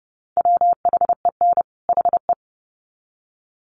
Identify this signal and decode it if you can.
Morse code, heard as a keyed tone, about 30 wpm, 700 Hz. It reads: WHEN HE
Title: Sets of 2 Words 30wpm